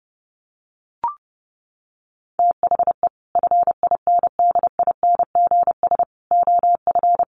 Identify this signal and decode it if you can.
Morse code, heard as a keyed tone, about 28 wpm, 700 Hz. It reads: THE FINDINGS OF
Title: Sets of 5 Words Encore2 30wpm